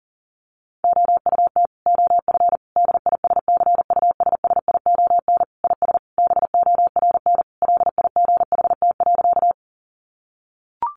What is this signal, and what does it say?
Morse code, heard as a keyed tone, about 40 wpm, 700 Hz. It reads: OUT OF DISCUSSION IS BORN LIGHT.
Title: Sentences from Top 1100 Words 40wpm